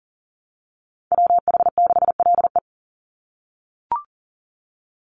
Morse code, heard as a keyed tone, about 39 wpm, 700 Hz. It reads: WH6LE
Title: NA SOTA Activator Callsigns 40wpm